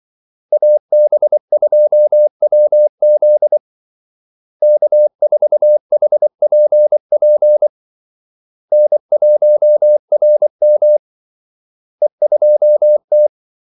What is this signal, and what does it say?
Morse code, heard as a keyed tone, about 24 wpm, 600 Hz. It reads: AB2WZ K4HPP N1RM E2T
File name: Callsigns-24wpm